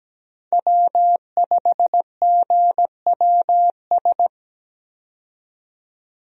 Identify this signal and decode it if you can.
Morse code, heard as a keyed tone, about 17 wpm, 700 Hz. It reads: W5GWS
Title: USA Call Signs - 5 Characters - 17wpm